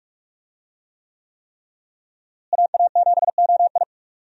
Morse code, heard as a keyed tone, about 46 wpm, 700 Hz. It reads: AA7OI
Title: USA Call Signs - 5 Characters - Repeated 3x - 45wpm